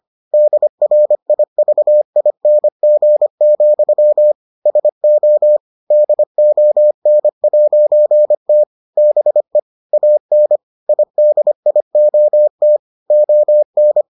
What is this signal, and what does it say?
Morse code, heard as a keyed tone, about 25 wpm, 600 Hz. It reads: DRIVING, SO DON'T BE AN IDIOT ON